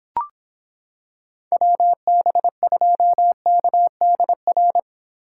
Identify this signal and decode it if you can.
Morse code, heard as a keyed tone, about 26 wpm, 700 Hz. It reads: WB2KDR